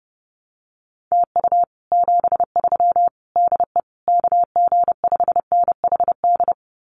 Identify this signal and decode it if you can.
Morse code, heard as a keyed tone, about 30 wpm, 700 Hz. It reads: TU 73 DE KG5NHD
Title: Common QSO Phrases Encore 30wpm